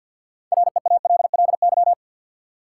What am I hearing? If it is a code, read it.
Morse code, heard as a keyed tone, about 50 wpm, 700 Hz. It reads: REALLY
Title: Six Letter Words - 50wpm